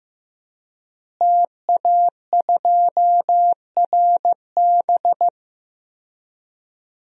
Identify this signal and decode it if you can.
Morse code, heard as a keyed tone, about 16 wpm, 700 Hz. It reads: TA2RB
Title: DXCC-100 Call Signs - 5 Characters - Repeated 2x - 15wpm